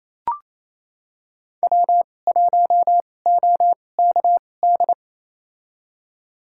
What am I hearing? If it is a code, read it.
Morse code, heard as a keyed tone, about 28 wpm, spaced as for 20 wpm, 700 Hz. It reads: W1OKD
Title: USA Call Signs - 5 Characters - 2x Character Spacing - 28wpm